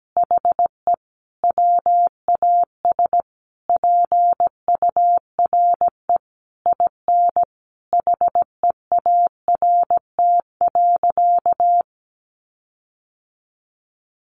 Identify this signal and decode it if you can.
Morse code, heard as a keyed tone, about 17 wpm, 700 Hz. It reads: HE WAS PURE IN HEART.
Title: Sentences from Top 1500 Words 17wpm